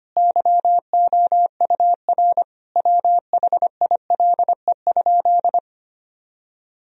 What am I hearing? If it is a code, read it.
Morse code, heard as a keyed tone, about 25 wpm, 700 Hz. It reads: YOUR WHILE?
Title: Sentences from Top 1000 Words - Review Part 1 - 25wpm